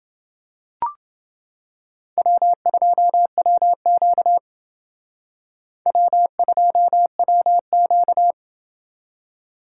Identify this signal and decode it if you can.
Morse code, heard as a keyed tone, about 29 wpm, 700 Hz. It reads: W2WQ W2WQ